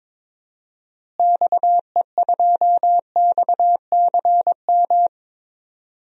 Morse code, heard as a keyed tone, about 22 wpm, 700 Hz. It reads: XE2XCM
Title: DXCC-100 Call Signs - 6 Characters - Repeated 3x - 22wpm